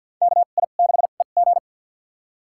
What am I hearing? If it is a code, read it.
Morse code, heard as a keyed tone, about 48 wpm, spaced as for 35 wpm, 700 Hz. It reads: KI6EG
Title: USA Call Signs - 5 Characters - 2x Character Spacing - 50wpm